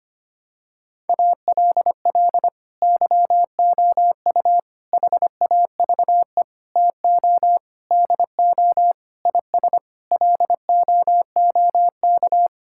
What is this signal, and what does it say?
Morse code, heard as a keyed tone, about 25 wpm, 700 Hz. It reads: ALL YOU HAVE TO DO IS LOOK
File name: Sets of 9 Words - 25wpm